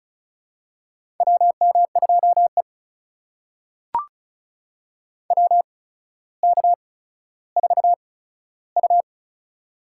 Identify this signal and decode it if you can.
Morse code, heard as a keyed tone, about 36 wpm, spaced as for 15 wpm, 700 Hz. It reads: WM2E WK4U
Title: USA Call Signs - 4 Characters - 8x Character Spacing - 35wpm